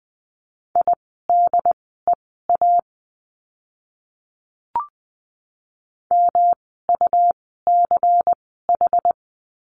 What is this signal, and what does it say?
Morse code, heard as a keyed tone, about 20 wpm, spaced as for 15 wpm, 700 Hz. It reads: IDEA MUCH
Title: 108 Lesson 18f - Letter m - Words 20wpm